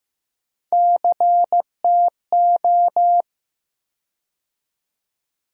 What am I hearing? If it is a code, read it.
Morse code, heard as a keyed tone, about 15 wpm, 700 Hz. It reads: CTO
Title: USA Call Sign Postfixes - 3 Letters 15wpm